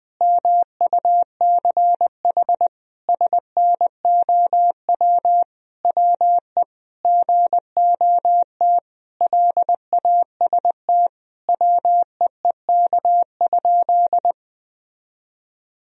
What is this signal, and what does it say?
Morse code, heard as a keyed tone, about 20 wpm, 700 Hz. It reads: MUCH SNOW WE GOT LAST WEEK?
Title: Sentences from Top 1000 Words - Review Part 3 - 20wpm